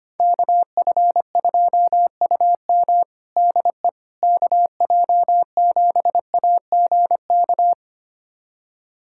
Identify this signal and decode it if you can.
Morse code, heard as a keyed tone, about 25 wpm, 700 Hz. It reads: KF2UM DE KJ7AGK
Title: USA Callsign DE Callsign 25wpm